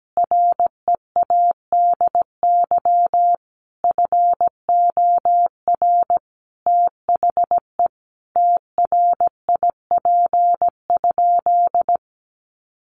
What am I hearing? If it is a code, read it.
Morse code, heard as a keyed tone, about 17 wpm, 700 Hz. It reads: READY FOR THE TRIP?
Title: Sentences from Top 1300 Words 17wpm